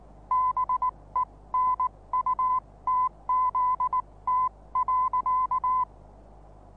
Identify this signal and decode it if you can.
Morse code, heard as a keyed tone, about 19 wpm, spaced as for 16 wpm, 1010 Hz. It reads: BENUTZT.
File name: repos / Morsekurs / DJ4UF Morsen26